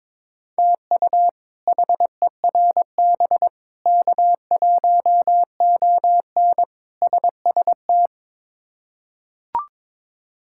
Morse code, heard as a keyed tone, about 23 wpm, 700 Hz. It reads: TU HERB K1ON SST
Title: K1USN SST QSO Practice - 22wpm